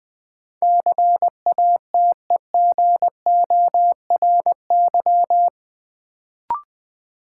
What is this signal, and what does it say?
Morse code, heard as a keyed tone, about 21 wpm, 700 Hz. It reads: CATEGORY
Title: Sentences from Top 1000 Words - Review Part 7 - New Word Drill - 20wpm